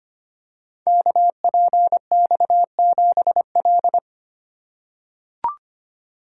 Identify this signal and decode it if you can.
Morse code, heard as a keyed tone, about 25 wpm, 700 Hz. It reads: KPX7L
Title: Code Groups - 5 Letter:Numbers - 25wpm